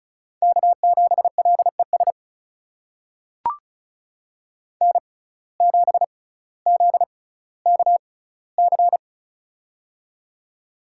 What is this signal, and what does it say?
Morse code, heard as a keyed tone, about 36 wpm, spaced as for 18 wpm, 700 Hz. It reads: K7LES N7ZKC
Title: USA Call Signs - 5 Characters - 6x Character Spacing - 35wpm